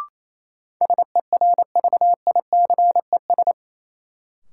Morse code, heard as a keyed tone, about 28 wpm, 700 Hz. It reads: SERVICES
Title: Sets of 5 Words - New Word Drill - 28wpm